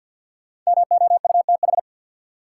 Morse code, heard as a keyed tone, about 53 wpm, 700 Hz. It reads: MOUTH
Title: Sentences from Top 1500 Words - New Word Drill - 50wpm